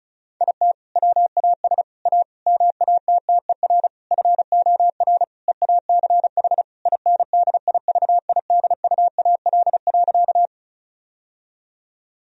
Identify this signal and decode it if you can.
Morse code, heard as a keyed tone, about 34 wpm, 700 Hz. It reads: IT WAS A MATTER FOR EACH INDIVIDUAL.
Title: Sentences from Top 1000 Words - Review Part 4 - 35wpm